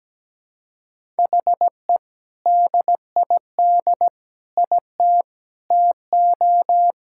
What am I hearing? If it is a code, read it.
Morse code, heard as a keyed tone, about 17 wpm, 700 Hz. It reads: HE DID IT TO
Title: Sentences from Top 1000 Words - Review Part 5 - 17wpm